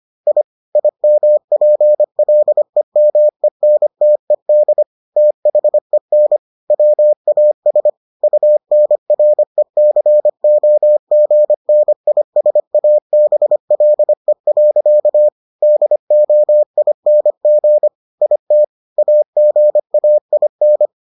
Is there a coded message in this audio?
Yes: Morse code from 0.3 to 20.9 s, heard as a keyed tone, about 25 wpm, 600 Hz. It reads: I IMPLEMENTED THEN WAS UNRECOGNISABLE. DOING IT AGAIN